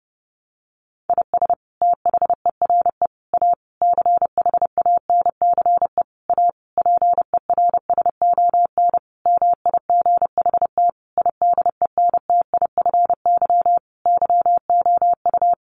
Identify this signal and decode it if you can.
Morse code, heard as a keyed tone, about 30 wpm, 700 Hz. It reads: IS THERE A CHANCE A PERSON MIGHT IDENTIFY YOU
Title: Sentences from Top 900 Words 30wpm